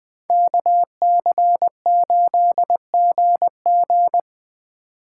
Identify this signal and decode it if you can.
Morse code, heard as a keyed tone, about 20 wpm, 700 Hz. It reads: KC8GG